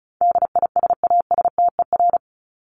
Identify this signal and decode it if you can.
Morse code, heard as a keyed tone, about 34 wpm, 700 Hz. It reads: DISASTER
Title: Top 1501-1750 Words 35wpm